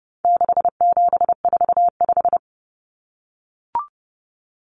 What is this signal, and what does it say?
Morse code, heard as a keyed tone, about 30 wpm, 700 Hz. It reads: 6745
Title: Numbers - 4-Digit 30wpm